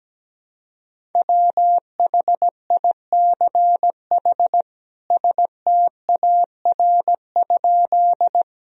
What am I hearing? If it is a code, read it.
Morse code, heard as a keyed tone, about 17 wpm, 700 Hz. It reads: WHICH STAR?
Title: Sentences from Top 1000 Words - Review Part 2 - 17wpm